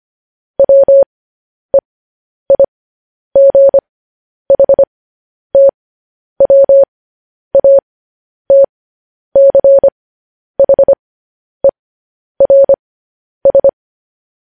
Morse code, heard as a keyed tone, about 25 wpm, spaced as for 10 wpm, 550 Hz. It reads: WEIGHTWATCHERS
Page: News Headlines in Morse Code at 10 WPM